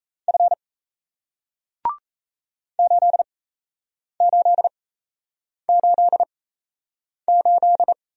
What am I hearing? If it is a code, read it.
Morse code, heard as a keyed tone, about 38 wpm, spaced as for 8 wpm, 700 Hz. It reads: F 8888